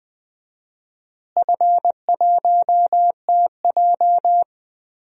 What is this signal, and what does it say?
Morse code, heard as a keyed tone, about 21 wpm, 700 Hz. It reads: F1TJ
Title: DXCC-100 Call Signs - 4 Characters - Repeated 3x - 20wpm